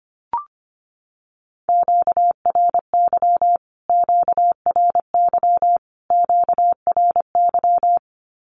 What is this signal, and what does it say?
Morse code, heard as a keyed tone, about 25 wpm, 700 Hz. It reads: QRY QRY QRY
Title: Q-Codes Repeated 3x 25wpm